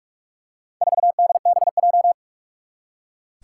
Morse code, heard as a keyed tone, about 41 wpm, 700 Hz. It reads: 4DBJ
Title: Four Challenging Letter-Number Combinations 45wpm